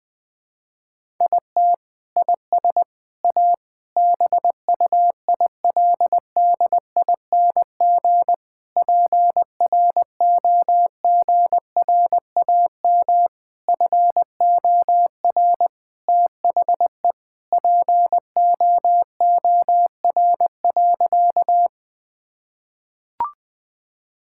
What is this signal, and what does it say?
Morse code, heard as a keyed tone, about 21 wpm, 700 Hz. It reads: IT IS A BUILDING PROGRAM FOR THE POOR.
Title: Sentences from Top 1000 Words - Review Part 5 - 20wpm